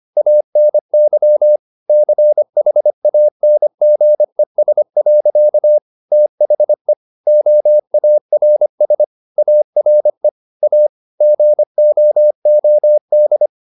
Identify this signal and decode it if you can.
Morse code, heard as a keyed tone, about 25 wpm, 600 Hz. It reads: ANY CHANGES. THE OARS ARE A GOOD